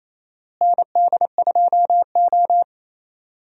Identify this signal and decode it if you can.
Morse code, heard as a keyed tone, about 28 wpm, 700 Hz. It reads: ND2O
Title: USA Call Signs - 4 Characters - Repeated 2x - 28wpm